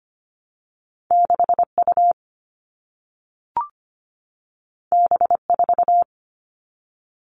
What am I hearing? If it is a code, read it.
Morse code, heard as a keyed tone, about 25 wpm, 700 Hz. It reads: 6U B4
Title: All Cut Numbers - 2-Digit 25wpm